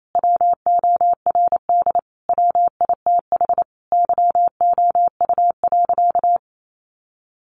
Morse code, heard as a keyed tone, about 28 wpm, 700 Hz. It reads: WORD WITH YOU.
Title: Sentences from Top 300 Words 28wpm